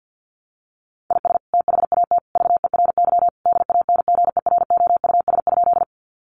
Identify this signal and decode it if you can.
Morse code, heard as a keyed tone, about 50 wpm, 700 Hz. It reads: IS THAT VERY DANGEROUS?
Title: Sentences from Top 1200 Words 50wpm